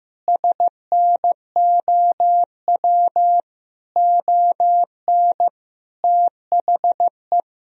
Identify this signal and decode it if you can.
Morse code, heard as a keyed tone, about 15 wpm, 700 Hz. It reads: SNOW ON THE
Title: Sentences from Top 1000 Words - Review Part 3 - 15wpm